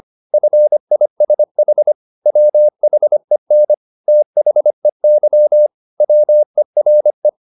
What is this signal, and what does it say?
Morse code, heard as a keyed tone, about 25 wpm, 600 Hz. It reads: FISH WHEN THEY WERE
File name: the old man and the sea part 1 25wpm